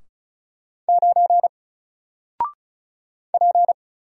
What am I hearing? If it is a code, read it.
Morse code, heard as a keyed tone, about 35 wpm, 700 Hz. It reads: 9 P
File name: Single-Letter-Number-35wpm